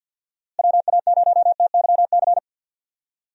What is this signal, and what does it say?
Morse code, heard as a keyed tone, about 49 wpm, 700 Hz. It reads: WA0TYC
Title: USA Call Signs - 6 Characters - Repeated 3x - 50wpm